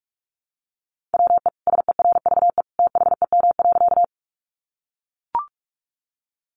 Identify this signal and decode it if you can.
Morse code, heard as a keyed tone, about 44 wpm, 700 Hz. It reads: WE SERVE THEM.
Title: Sentences from Top 400 Words 45wpm